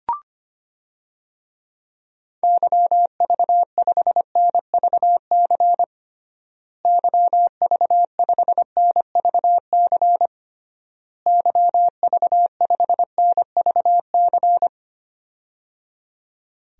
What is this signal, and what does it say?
Morse code, heard as a keyed tone, about 25 wpm, 700 Hz. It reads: YV5NVC YV5NVC YV5NVC
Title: DXCC-100 Call Signs - 6 Characters - Repeated 3x - 25wpm